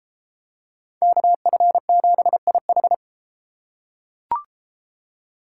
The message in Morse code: KF7IH

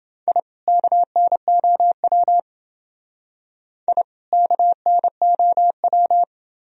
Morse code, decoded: I KNOW I KNOW

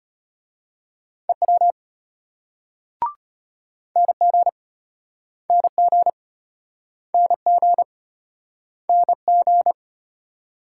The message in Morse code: EW NG NG NG NG

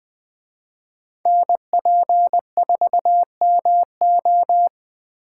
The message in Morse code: NP4MO

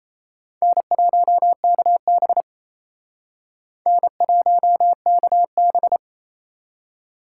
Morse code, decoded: N1KB N1KB

